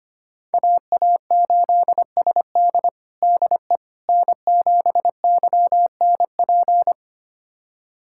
Morse code, heard as a keyed tone, about 25 wpm, 700 Hz. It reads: AA8SD DE N7YNP
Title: USA Callsign DE Callsign - 5 Characters - 25wpm